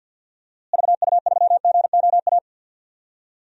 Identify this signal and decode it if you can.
Morse code, heard as a keyed tone, about 47 wpm, 700 Hz. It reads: VU3GOA
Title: DXCC-100 Call Signs - 6 Characters - Repeated 2x - 50wpm